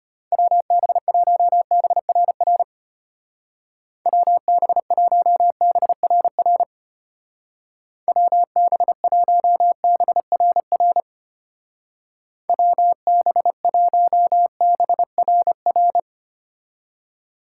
Morse code, heard as a keyed tone, about 32 wpm, 700 Hz. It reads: WB1BRR WB1BRR WB1BRR WB1BRR